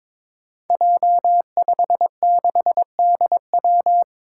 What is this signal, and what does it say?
Morse code, heard as a keyed tone, about 22 wpm, 700 Hz. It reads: J56DW
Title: Five Challenging Letter-Number Combinations 22wpm